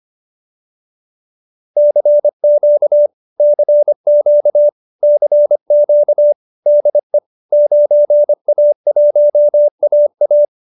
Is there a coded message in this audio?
Yes: Morse code from 1.8 to 10.4 s, heard as a keyed tone, about 25 wpm, 590 Hz. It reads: CQ CQ CQ DE 9A1AA